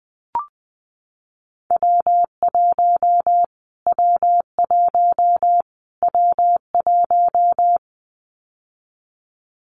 Morse code, heard as a keyed tone, about 20 wpm, 700 Hz. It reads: W1 W1 W1
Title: Callsign Prefixes for Top 25 DXCC - Repeated 3x - 20wpm